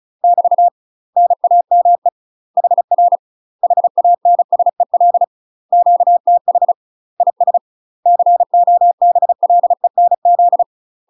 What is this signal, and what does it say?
Morse code, heard as a keyed tone, about 36 wpm, 700 Hz. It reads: <BT> NAME HR HANSEL QTH IS COBLENZ
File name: UK MASTER QSOs TRAINING / 35WPM / UK MASTER QSOs 0009